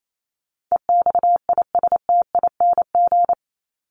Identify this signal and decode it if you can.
Morse code, heard as a keyed tone, about 28 wpm, 700 Hz. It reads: EXISTING
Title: Top 1251-1500 Words 28wpm